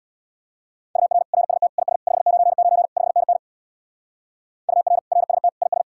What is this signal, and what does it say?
Morse code, heard as a keyed tone, about 59 wpm, 700 Hz. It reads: UR RST IS 599 5NN UR RST IS
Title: Speed-Racing Common QSO Phrases - 50wpm